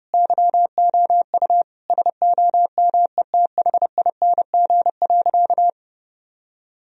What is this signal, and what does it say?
Morse code, heard as a keyed tone, about 32 wpm, 700 Hz. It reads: YOU SOMETHING.
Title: Sentences from Top 1000 Words - Review Part 3 - 30wpm